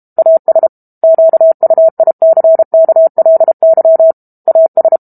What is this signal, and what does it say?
Morse code, heard as a keyed tone, about 32 wpm, 650 Hz. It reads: AS QUICKLY AS
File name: Book Two / 06-Chapter 6 - 32 WPM